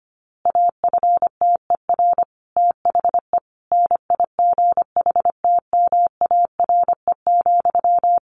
Morse code, heard as a keyed tone, about 25 wpm, 700 Hz. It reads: AFTER THE NIGHTMARE,
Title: A Sound Of Thunder 25wpm